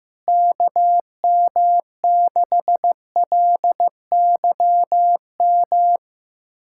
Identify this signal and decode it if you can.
Morse code, heard as a keyed tone, about 14 wpm, 700 Hz. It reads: KM6LYM
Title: USA Callsign DE Callsign 15wpm